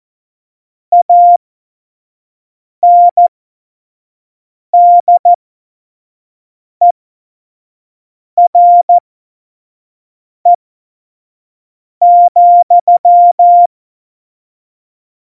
Morse code, heard as a keyed tone, about 14 wpm, spaced as for 5 wpm, 700 Hz. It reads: ANDERE,
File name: Morsekurs / Witze / Witz 26. September 17, Tempo 25